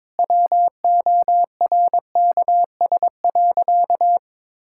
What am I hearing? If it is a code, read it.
Morse code, heard as a keyed tone, about 22 wpm, 700 Hz. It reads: WORKS.
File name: practice / 22wpm / Sentences with 2-Word Phrases - 22wpm